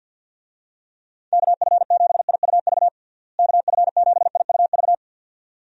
Morse code, heard as a keyed tone, about 49 wpm, 700 Hz. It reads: KF7IUV KF7IUV
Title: USA Call Signs - 6 Characters - Repeated 2x - 50wpm